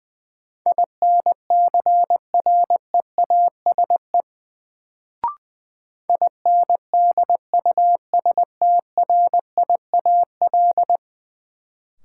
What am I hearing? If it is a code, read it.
Morse code, heard as a keyed tone, about 21 wpm, 700 Hz. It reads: INCREASE INDUSTRIAL
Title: Sets of 5 Words - New Word Drill - 20wpm